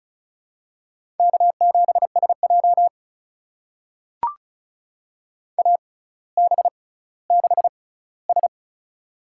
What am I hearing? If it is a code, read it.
Morse code, heard as a keyed tone, about 36 wpm, spaced as for 18 wpm, 700 Hz. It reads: K7SJ AB6S
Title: USA Call Signs - 4 Characters - 6x Character Spacing - 35wpm